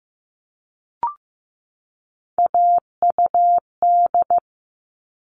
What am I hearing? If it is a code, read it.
Morse code, heard as a keyed tone, about 15 wpm, 700 Hz. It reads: AUD
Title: All Cut Numbers - 3-Digit 15wpm